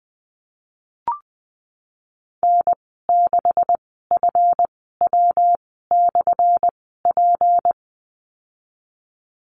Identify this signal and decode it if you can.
Morse code, heard as a keyed tone, about 19 wpm, spaced as for 14 wpm, 700 Hz. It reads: N6FW/P
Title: 179 Lesson 29g - Slash - Call Signs 20wpm